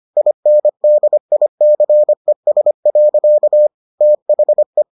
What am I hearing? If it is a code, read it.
Morse code, heard as a keyed tone, about 25 wpm, 600 Hz. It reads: INDICES. THE